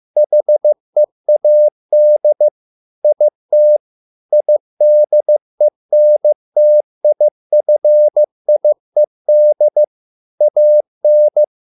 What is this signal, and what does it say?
Morse code, heard as a keyed tone, about 15 wpm, 600 Hz. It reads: HEAD IT IDENTIFIED AN